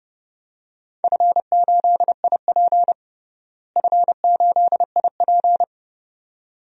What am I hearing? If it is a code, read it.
Morse code, heard as a keyed tone, about 30 wpm, 700 Hz. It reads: F8IP F8IP
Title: DXCC-100 Call Signs - 4 Characters - Repeated 2x - 30wpm